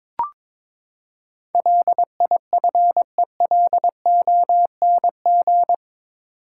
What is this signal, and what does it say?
Morse code, heard as a keyed tone, about 21 wpm, 700 Hz. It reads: LIFELONG
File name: practice / 22wpm / Eight Letter Words - 22wpm